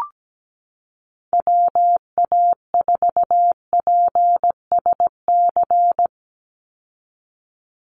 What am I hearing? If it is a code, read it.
Morse code, heard as a keyed tone, about 17 wpm, 700 Hz. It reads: WA4PSC
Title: Call Signs - 17wpm